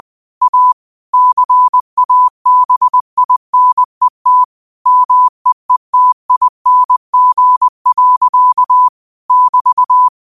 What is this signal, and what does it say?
Morse code, heard as a keyed tone, about 20 wpm, 1000 Hz. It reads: A CABINET MEETING. <BT>
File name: qotdfiles / 20wpm